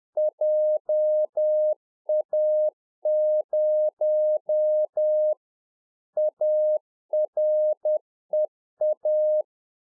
Morse code, heard as a keyed tone, about 10 wpm, 610 Hz. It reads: JA0 AREA